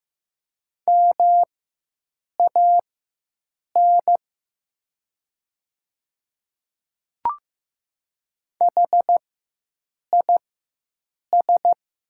Morse code, heard as a keyed tone, about 15 wpm, spaced as for 7 wpm, 690 Hz. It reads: MAN HIS
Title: 4x Character Spacing - Top 500 Words - 15wpm